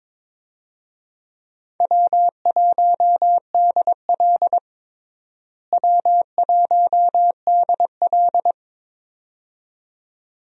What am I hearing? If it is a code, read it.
Morse code, heard as a keyed tone, about 21 wpm, 700 Hz. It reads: W1DL W1DL
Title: USA Call Signs - 4 Characters - Repeated 2x - 22wpm